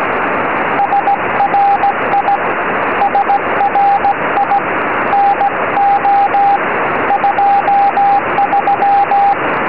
Morse code, heard as a keyed tone, about 16 wpm, 800 Hz, with radio noise. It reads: SRI SRI NO 23
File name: tropo / la2pha 70cm